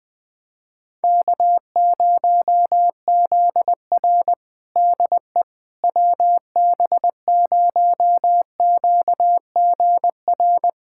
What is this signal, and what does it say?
Morse code, heard as a keyed tone, about 20 wpm, 700 Hz. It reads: K0ZR DE WB0QGR
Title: Practice QSO - Exchange 4 - Ending - 20wpm